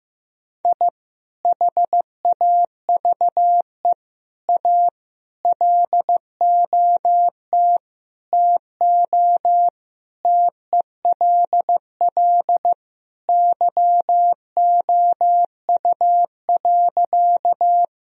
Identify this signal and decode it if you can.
Morse code, heard as a keyed tone, about 15 wpm, 700 Hz. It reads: I HAVE A LOT TO TELL YOU.
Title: Sentences from Top 1000 Words - Review Part 3 - 15wpm